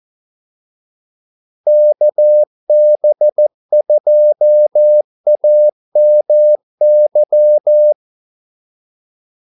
Morse code, heard as a keyed tone, about 14 wpm, 600 Hz. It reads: KB2AMY
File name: Callsigns-14wpm